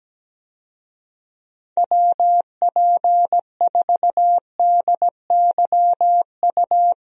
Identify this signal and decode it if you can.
Morse code, heard as a keyed tone, about 17 wpm, 700 Hz. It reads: WP4DYU
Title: USA Call Signs - 6 Characters - Repeated 2x - 17wpm